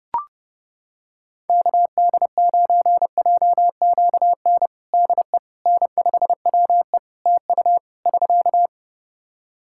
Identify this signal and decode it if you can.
Morse code, heard as a keyed tone, about 30 wpm, 700 Hz. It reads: KD9JQN DE N5WE TU <SK>
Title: Practice QSO - Exchange 4 - Ending - 30wpm